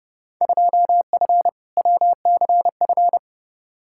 Morse code, heard as a keyed tone, about 30 wpm, 700 Hz. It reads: 2F WCF
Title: Field Day Exchanges Encore - 30wpm